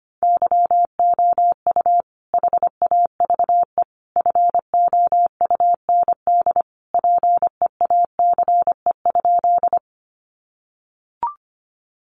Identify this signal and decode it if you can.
Morse code, heard as a keyed tone, about 25 wpm, 700 Hz. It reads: YOU HAVE FOUND PEACE?
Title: Sentences from Top 1500 Words 25wpm